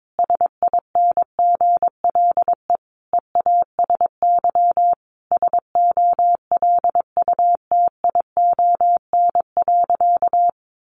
Morse code, heard as a keyed tone, about 22 wpm, 700 Hz. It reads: SINGLE EASY SOLUTION.
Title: Sentences from Top 1300 Words 22wpm